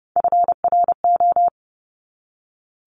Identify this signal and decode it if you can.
Morse code, heard as a keyed tone, about 30 wpm, 690 Hz. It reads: FRO
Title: Top Three Letter Combinations in Words 30wpm